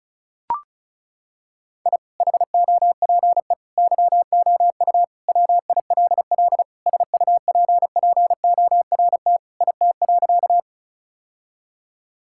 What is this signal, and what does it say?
Morse code, heard as a keyed tone, about 36 wpm, 700 Hz. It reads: I HOPE YOU WILL SUPPORT IT.
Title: Sentences from Top 1000 Words - Review Part 5 - 35wpm